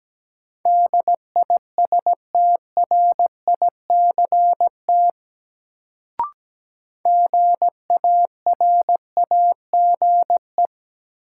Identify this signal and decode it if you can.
Morse code, heard as a keyed tone, about 17 wpm, 700 Hz. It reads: DISTRICT GARAGE
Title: Sentences from Top 1600 Words - New Word Drill Alt - 17wpm